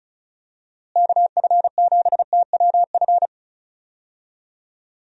Morse code, heard as a keyed tone, about 34 wpm, 700 Hz. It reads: KF7TWF
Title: USA Call Signs - 6 Characters - Repeated 3x - 35wpm